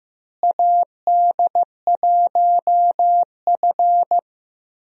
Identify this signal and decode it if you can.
Morse code, heard as a keyed tone, about 16 wpm, 700 Hz. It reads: AD1F
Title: USA Call Signs - 4 Characters - Repeated 2x - 15wpm